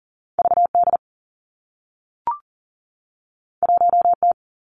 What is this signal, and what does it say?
Morse code, heard as a keyed tone, about 40 wpm, 700 Hz. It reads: VD 1T
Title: Two Challenging Letter-Number Combinations 40wpm